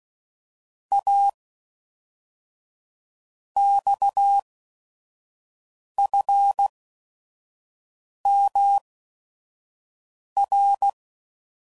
Morse code, heard as a keyed tone, about 16 wpm, spaced as for 5 wpm, 800 Hz. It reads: A XFMR
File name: CW Code Groups @5WPM